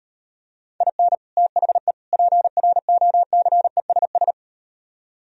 Morse code, heard as a keyed tone, about 38 wpm, 700 Hz. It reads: IN THE PROCESS